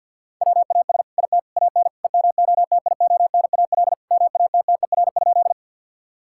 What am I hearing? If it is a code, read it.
Morse code, heard as a keyed tone, about 46 wpm, 700 Hz. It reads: WAS IT AN EMOTIONAL MATTER?
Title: Sentences from Top 1000 Words - Review Part 5 - 50wpm